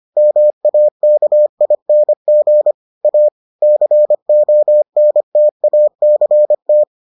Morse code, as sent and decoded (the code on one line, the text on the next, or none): -- .- -.- .. -. --. / .- / -.-. --- -. - .- -.-. -
MAKING A CONTACT